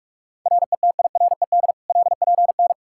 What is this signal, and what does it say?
Morse code, heard as a keyed tone, about 46 wpm, 700 Hz. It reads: RETIRED LPN
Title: Practice QSO - Exchange 3 - More Info - 45wpm